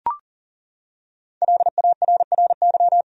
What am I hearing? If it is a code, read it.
Morse code, heard as a keyed tone, about 40 wpm, 700 Hz. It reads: LARRY
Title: USA Common Names - Max 5 Letters - 40wpm